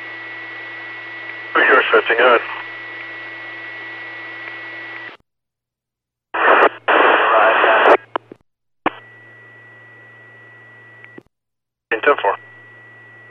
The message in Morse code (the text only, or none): none